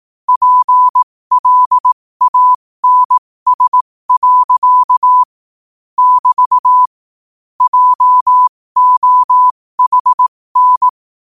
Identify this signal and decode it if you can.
Morse code, heard as a keyed tone, about 18 wpm, spaced as for 15 wpm, 1000 Hz. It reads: PLANS. <BT> JOHN